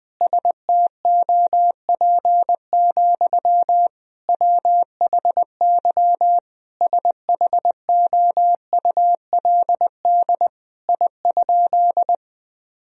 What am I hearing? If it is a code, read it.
Morse code, heard as a keyed tone, about 20 wpm, 700 Hz. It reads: STOP, WHY SHOULD I?
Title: Sentences from Top 1000 Words - Review Part 1 - 20wpm